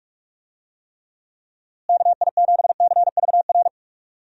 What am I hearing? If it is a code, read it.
Morse code, heard as a keyed tone, about 46 wpm, 700 Hz. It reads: KI7CVR
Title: USA Call Signs - 6 Characters - Repeated 2x - 45wpm